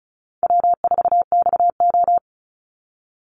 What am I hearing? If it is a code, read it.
Morse code, heard as a keyed tone, about 34 wpm, 700 Hz. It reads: W4XO
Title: Easy Call Signs Repeated 3x - 35wpm